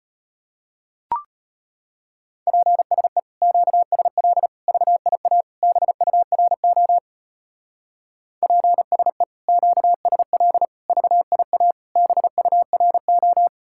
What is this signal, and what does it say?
Morse code, heard as a keyed tone, about 32 wpm, 700 Hz. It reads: PSE QSL VIA BURO PSE QSL VIA BURO